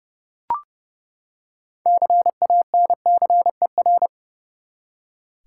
Morse code, heard as a keyed tone, about 29 wpm, 700 Hz. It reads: CANCER
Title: Six Letter Words - 30wpm